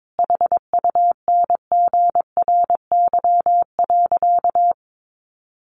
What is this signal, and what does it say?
Morse code, heard as a keyed tone, about 22 wpm, 700 Hz. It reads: HUNGRY.
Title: Sentences from Top 1400 Words 22wpm